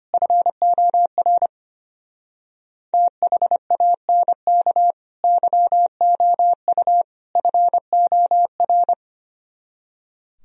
Morse code, heard as a keyed tone, about 32 wpm, 700 Hz. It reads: FOR THANK YOU FOR